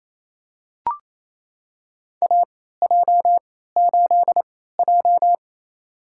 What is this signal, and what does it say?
Morse code, heard as a keyed tone, about 28 wpm, spaced as for 16 wpm, 700 Hz. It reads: AJ8J
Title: USA Call Signs - 4 Characters - 3x Character Spacing - 28wpm